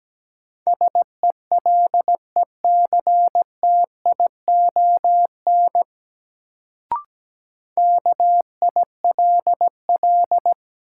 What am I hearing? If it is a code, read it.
Morse code, heard as a keyed tone, about 17 wpm, 700 Hz. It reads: SELECTION KILL
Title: Sentences from Top 1400 Words - New Word Drill - 17wpm